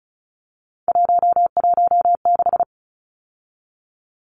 Morse code, heard as a keyed tone, about 34 wpm, 700 Hz. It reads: 116
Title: Numbers - 3-Digit 35wpm